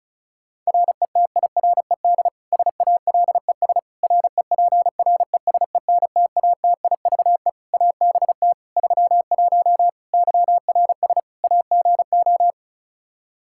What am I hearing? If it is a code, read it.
Morse code, heard as a keyed tone, about 36 wpm, 700 Hz. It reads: RETIRED SALES REPRESENTATIVE ABT 31 YRS AGO
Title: Practice QSO - Exchange 3 - More Info - 35wpm